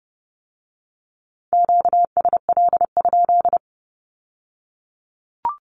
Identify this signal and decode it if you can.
Morse code, heard as a keyed tone, about 30 wpm, 700 Hz. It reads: QSL?
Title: Q-Codes with Spoken Meaning - 30wpm